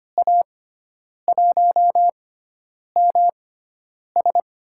Morse code, heard as a keyed tone, about 25 wpm, spaced as for 9 wpm, 700 Hz. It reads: A1MS
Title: USA Call Signs - 6 Characters - 6x Character Spacing - 25wpm